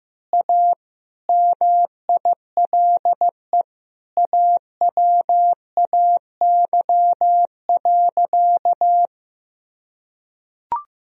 Morse code, heard as a keyed tone, about 16 wpm, 700 Hz. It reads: A MILE AWAY.
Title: Sentences from Top 1000 Words - Review Part 2 - 15wpm